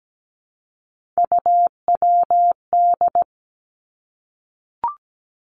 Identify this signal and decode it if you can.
Morse code, heard as a keyed tone, about 17 wpm, 700 Hz. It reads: UWD
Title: Three Challenging Letter-Number Combinations 17wpm